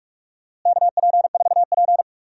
Morse code, heard as a keyed tone, about 46 wpm, 700 Hz. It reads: KP4P
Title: DXCC-100 Call Signs - 4 Characters - Repeated 2x - 45wpm